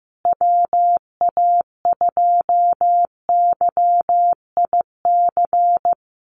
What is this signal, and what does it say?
Morse code, heard as a keyed tone, about 15 wpm, 700 Hz. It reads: WA2YIC
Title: USA Call Signs Repeated 3x 15wpm